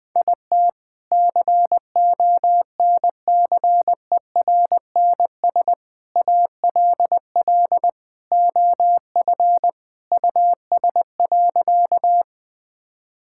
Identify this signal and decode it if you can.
Morse code, heard as a keyed tone, about 21 wpm, 700 Hz. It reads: IT CONCERNS ALL OF US.
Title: Sentences with 3-Word Phrases - 20wpm